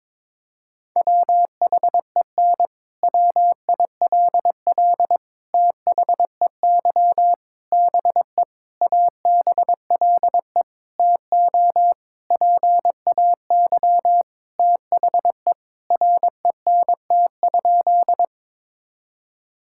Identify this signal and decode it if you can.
Morse code, heard as a keyed tone, about 22 wpm, 700 Hz. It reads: WHEN WILL THEY BE ABLE TO PAY THE RENT?
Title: Sentences from Top 1000 Words - Review Part 5 - 22wpm